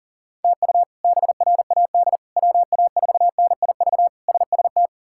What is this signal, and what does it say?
Morse code, heard as a keyed tone, about 40 wpm, 700 Hz. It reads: TU BRAD WA4NIV SST
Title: K1USN SST QSO Practice - 40wpm